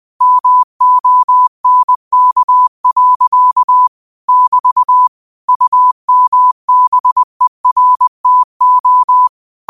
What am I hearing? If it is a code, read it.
Morse code, heard as a keyed tone, about 20 wpm, 1000 Hz. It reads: MONK. <BT> UMBERTO